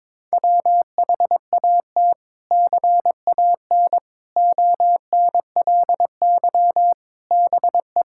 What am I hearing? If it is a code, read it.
Morse code, heard as a keyed tone, about 22 wpm, 700 Hz. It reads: WHAT CAN ONLY BE
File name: practice / 22wpm / Sets of 6 Words - 22wpm